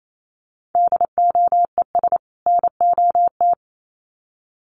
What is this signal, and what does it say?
Morse code, heard as a keyed tone, about 28 wpm, 700 Hz. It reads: DOES NOT
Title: Sets of 2 Words 28wpm